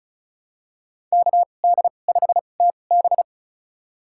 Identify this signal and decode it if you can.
Morse code, heard as a keyed tone, about 36 wpm, spaced as for 25 wpm, 700 Hz. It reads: KD5TB
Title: USA Call Signs - 5 Characters - 2x Character Spacing - 35wpm